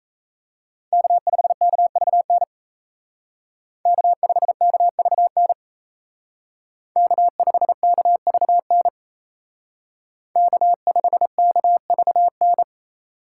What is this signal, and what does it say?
Morse code, heard as a keyed tone, about 36 wpm, 700 Hz. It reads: K5KVN K5KVN K5KVN K5KVN